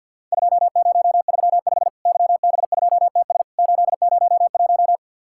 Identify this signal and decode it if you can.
Morse code, heard as a keyed tone, about 50 wpm, 700 Hz. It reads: 2035 YB2TS 801
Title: Worked All Europe DX Contest - QTC Exchange with Repeats - 50wpm